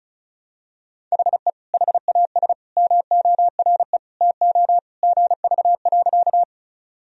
Morse code, heard as a keyed tone, about 35 wpm, 700 Hz. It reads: HE HAS MORE TO GV.
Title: Sentences from Top 200 Words and Abbreviations 35wpm